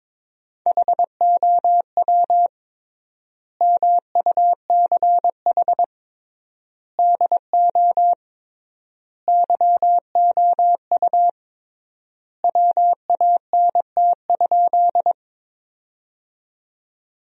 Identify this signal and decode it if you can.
Morse code, heard as a keyed tone, about 22 wpm, 700 Hz. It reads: HOW MUCH DO YOU WANT?
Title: 3x Word-Spacing - Sentences from Top 200 Words - 22wpm